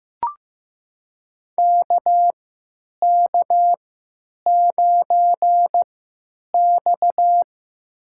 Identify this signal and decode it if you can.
Morse code, heard as a keyed tone, about 15 wpm, spaced as for 9 wpm, 700 Hz. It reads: KK9X
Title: USA Call Signs - 4 Characters - 3x Character Spacing - 15wpm